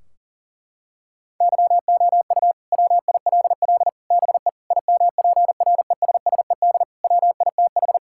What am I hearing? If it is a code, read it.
Morse code, heard as a keyed tone, about 40 wpm, 700 Hz. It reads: YOU WILL BE IMPRESSED WITH
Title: Sentences from Top 1600 Words 40wpm